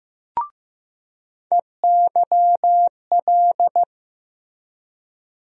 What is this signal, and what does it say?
Morse code, heard as a keyed tone, about 16 wpm, 700 Hz. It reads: EYL
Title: USA Call Sign Postfixes - 3 Letters 15wpm